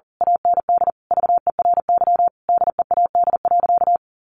Morse code, heard as a keyed tone, about 40 wpm, 700 Hz. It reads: AND VERY DEAD.
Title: A Sound Of Thunder 40wpm